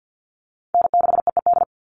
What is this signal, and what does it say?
Morse code, heard as a keyed tone, about 50 wpm, 700 Hz. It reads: N6EED
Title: Call Signs - 50wpm